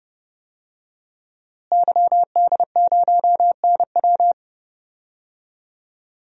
Morse code, heard as a keyed tone, about 32 wpm, 700 Hz. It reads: YD0NW
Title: DXCC-100 Call Signs - 5 Characters - Repeated 3x - 30wpm